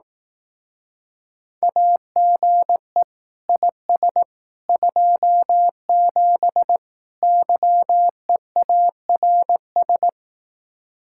Morse code, heard as a keyed tone, about 18 wpm, 700 Hz. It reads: AGE IS 27 YEARS
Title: Speed-Racing Common QSO Phrases Encore - 15wpm